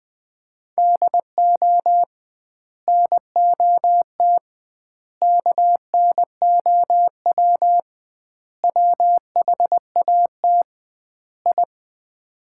Morse code, heard as a keyed tone, about 20 wpm, 700 Hz. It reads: DO NOT KNOW WHAT I
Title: 2x Word-Spacing - Sets of 5 Words - 20wpm